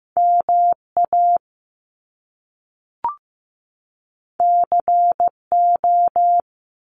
Morse code, heard as a keyed tone, about 15 wpm, 700 Hz. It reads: MA CO
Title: US States 15wpm